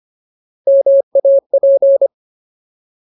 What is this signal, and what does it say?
Morse code, heard as a keyed tone, about 25 wpm, 550 Hz. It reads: MAP
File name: Ultimate Word List / 25WPM Ultimate